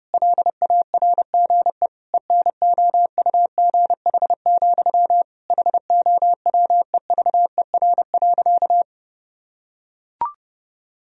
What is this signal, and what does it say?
Morse code, heard as a keyed tone, about 30 wpm, 700 Hz. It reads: LARGE ENOUGH, HOWEVER.
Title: Sentences from Top 1000 Words - Review Part 1 - 30wpm